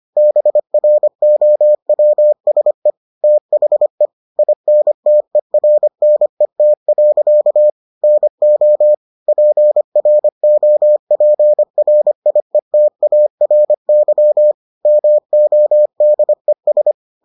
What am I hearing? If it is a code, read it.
Morse code, heard as a keyed tone, about 25 wpm, 600 Hz. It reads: BROWSE THE INTERNET. NO PROPRIETARY MODES